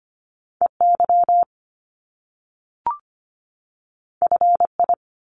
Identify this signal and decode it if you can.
Morse code, heard as a keyed tone, about 25 wpm, 700 Hz. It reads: EY FI
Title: Top Two Letter Combinations in Words 25wpm